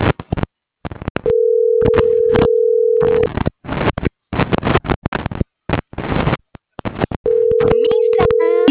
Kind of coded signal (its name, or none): telephone call progress tone